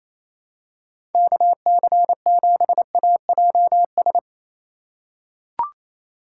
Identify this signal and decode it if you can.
Morse code, heard as a keyed tone, about 28 wpm, 700 Hz. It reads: KC7AJS
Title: USA Call Signs - 6 Characters - 6x Character Spacing - 28wpm